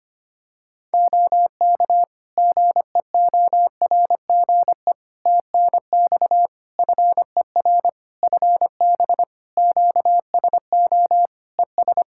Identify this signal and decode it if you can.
Morse code, heard as a keyed tone, about 25 wpm, 700 Hz. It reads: OK GEORGE TNX FER FB QSO ES
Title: K1USN SST QSO Practice 25wpm